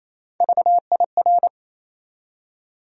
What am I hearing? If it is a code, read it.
Morse code, heard as a keyed tone, about 28 wpm, 700 Hz. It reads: VIR
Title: USA Call Sign Postfixes - 3 Letters 28wpm